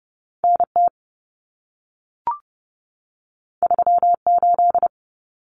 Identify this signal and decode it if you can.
Morse code, heard as a keyed tone, about 30 wpm, 700 Hz. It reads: NT 38
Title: Common Cut Numbers - 2-Digit 30wpm